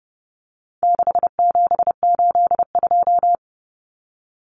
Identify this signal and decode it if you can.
Morse code, heard as a keyed tone, about 30 wpm, 700 Hz. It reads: 6782
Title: Common Cut Numbers - 4-Digit 30wpm